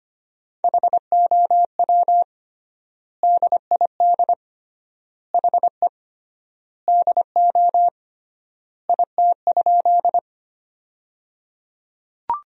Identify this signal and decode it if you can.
Morse code, heard as a keyed tone, about 25 wpm, 700 Hz. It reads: HOW DID HE DO IT?
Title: 3x Word-Spacing - Sentences from Top 100 Words - 25wpm